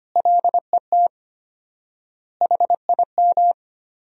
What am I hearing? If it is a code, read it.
Morse code, heard as a keyed tone, about 25 wpm, 700 Hz. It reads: LET HIM
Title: 4x Word-Spacing - Sentences from Top 300 Words - 25wpm